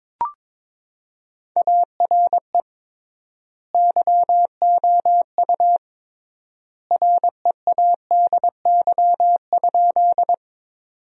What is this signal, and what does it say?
Morse code, heard as a keyed tone, about 22 wpm, 700 Hz. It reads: ARE YOU READY?
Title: 3x Word-Spacing - Sentences from Top 400 Words - 22wpm